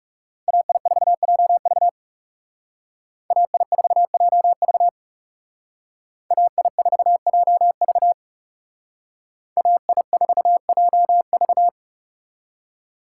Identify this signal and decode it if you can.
Morse code, heard as a keyed tone, about 39 wpm, 700 Hz. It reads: AI4JV AI4JV AI4JV AI4JV